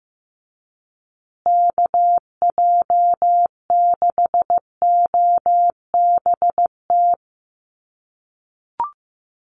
Morse code, heard as a keyed tone, about 15 wpm, 700 Hz. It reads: KJ6OBT
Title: USA Call Signs Repeated 3x 15wpm